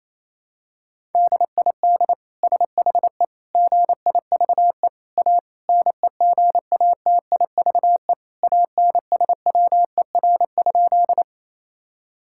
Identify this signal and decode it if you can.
Morse code, heard as a keyed tone, about 27 wpm, 700 Hz. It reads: DID SHE GIVE A NEGATIVE ANSWER?
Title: Sentences from Top 1000 Words - Review Part 6 - 28wpm